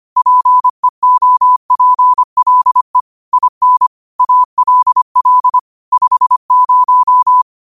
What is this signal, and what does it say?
Morse code, heard as a keyed tone, about 25 wpm, 1000 Hz. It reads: PEOPLE IN ALL 50
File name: qotdfiles / 25wpm